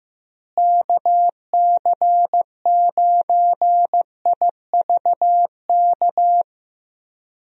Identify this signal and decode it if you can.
Morse code, heard as a keyed tone, about 15 wpm, 700 Hz. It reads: KC9IVK
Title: USA Call Signs - 6 Characters - Repeated 3x - 15wpm